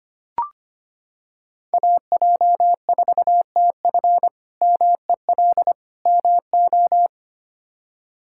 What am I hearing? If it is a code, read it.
Morse code, heard as a keyed tone, about 25 wpm, 700 Hz. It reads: AJ4TF MEL MO
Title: K1USN SST QSO Practice - 25wpm